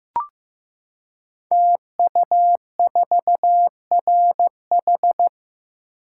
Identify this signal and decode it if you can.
Morse code, heard as a keyed tone, about 15 wpm, 700 Hz. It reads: TU4RH
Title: Code Groups - 5 Letter:Numbers - 15wpm